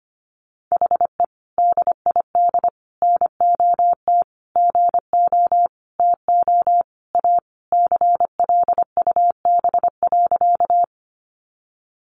Morse code, heard as a keyed tone, about 25 wpm, 700 Hz. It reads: HE DID NOT GO TO A CLUB.
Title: Sentences from Top 1200 Words 25wpm